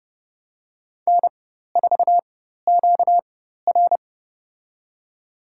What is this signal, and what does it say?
Morse code, heard as a keyed tone, about 30 wpm, spaced as for 14 wpm, 700 Hz. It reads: N4QR
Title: USA Call Signs - 4 Characters - 4x Character Spacing - 30wpm